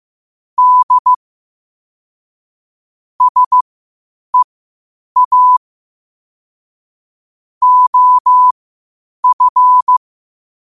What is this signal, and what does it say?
Morse code, heard as a keyed tone, about 15 wpm, spaced as for 8 wpm, 1000 Hz. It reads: D SEA OF
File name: 8wpm